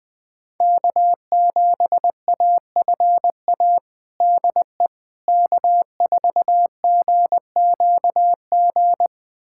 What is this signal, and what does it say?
Morse code, heard as a keyed tone, about 20 wpm, 700 Hz. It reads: K7AFA DE K4GQG
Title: USA Callsign DE Callsign - 5 Characters - 20wpm